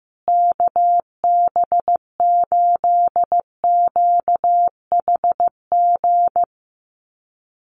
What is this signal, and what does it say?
Morse code, heard as a keyed tone, about 14 wpm, 700 Hz. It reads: KB8QHG
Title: USA Call Signs Repeated 2x 15wpm